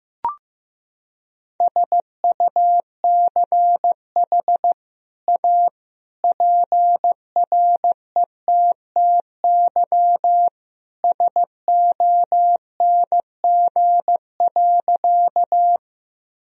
Morse code, heard as a keyed tone, about 15 wpm, 700 Hz. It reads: SUCH A PRETTY SONG.
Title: Sentences from Top 1000 Words - Review Part 2 - 15wpm